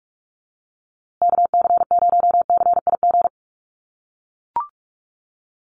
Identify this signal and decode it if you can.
Morse code, heard as a keyed tone, about 44 wpm, 700 Hz. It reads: KC0CIG
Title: USA Call Signs Repeated 2x 45wpm